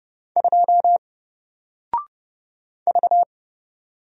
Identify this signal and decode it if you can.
Morse code, heard as a keyed tone, about 29 wpm, 700 Hz. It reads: 2 V
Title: Single Character (Letter:Number:Punctuation:Prosign) 30wpm